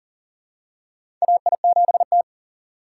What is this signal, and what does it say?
Morse code, heard as a keyed tone, about 39 wpm, 700 Hz. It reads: AI7T
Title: USA Call Signs - 4 Characters - 2x Character Spacing - 40wpm